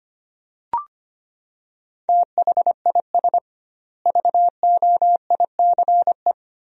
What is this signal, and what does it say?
Morse code, heard as a keyed tone, about 25 wpm, 700 Hz. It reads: THIS VOICE
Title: 2x Word-Spacing - Sentences from Top 500 Words - 25wpm